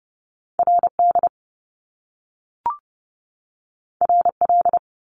Morse code, heard as a keyed tone, about 30 wpm, 700 Hz. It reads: RD RL
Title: Top Two Letter Combinations in Words 30wpm